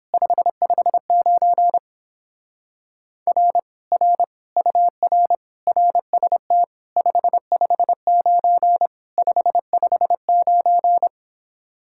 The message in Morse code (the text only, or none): 559 R R UR RST 559 559